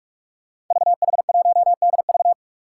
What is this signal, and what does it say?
Morse code, heard as a keyed tone, about 44 wpm, 700 Hz. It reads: VH1DV
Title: Five Challenging Letter-Number Combinations 45wpm